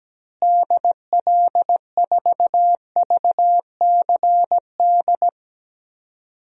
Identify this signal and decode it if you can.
Morse code, heard as a keyed tone, about 17 wpm, 700 Hz. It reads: DL4VCD
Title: DXCC-100 Call Signs - 6 Characters - Repeated 2x - 17wpm